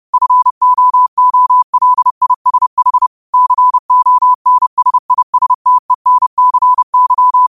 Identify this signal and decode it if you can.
Morse code, heard as a keyed tone, about 30 wpm, 1000 Hz. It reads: FOOLISH CONSISTENCY
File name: qotdfiles / 30wpm